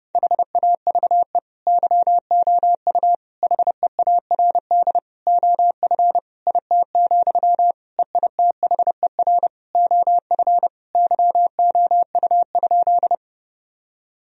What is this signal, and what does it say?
Morse code, heard as a keyed tone, about 30 wpm, 700 Hz. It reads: HAVE YOU HEARD OF IT, EITHER OF YOU?
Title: Sentences from Top 1000 Words - Review Part 4 - 30wpm